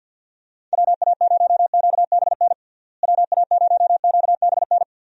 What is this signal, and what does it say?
Morse code, heard as a keyed tone, about 50 wpm, 700 Hz. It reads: WA0QBN WA0QBN
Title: USA Call Signs - 6 Characters - Repeated 3x - 50wpm